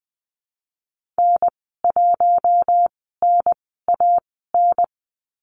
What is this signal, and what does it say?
Morse code, heard as a keyed tone, about 20 wpm, 700 Hz. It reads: N 1 N A N
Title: 055 Lesson 10g - Letter r - Call Signs 20wpm